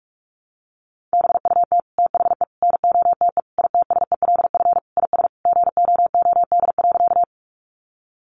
Morse code, heard as a keyed tone, about 44 wpm, 700 Hz. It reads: BUT THE NOTE ITSELF IS GOOD.
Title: Sentences from Top 800 Words 45wpm